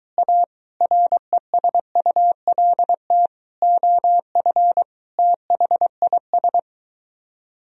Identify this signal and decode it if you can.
Morse code, heard as a keyed tone, about 23 wpm, 700 Hz. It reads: A RESULT OF THIS